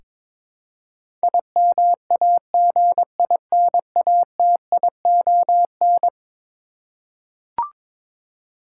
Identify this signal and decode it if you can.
Morse code, heard as a keyed tone, about 22 wpm, 700 Hz. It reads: IMAGINATION
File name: practice / 22wpm / Eleven Letter Words - 22wpm